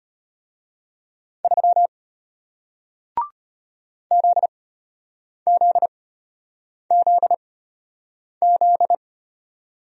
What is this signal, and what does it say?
Morse code, heard as a keyed tone, about 33 wpm, spaced as for 8 wpm, 700 Hz. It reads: 3 ZZZZ